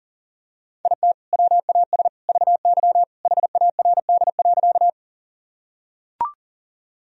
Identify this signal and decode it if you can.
Morse code, heard as a keyed tone, about 40 wpm, 700 Hz. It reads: IT WAS VY HARD.
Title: Sentences from Top 300 Words and Abbreviations 40wpm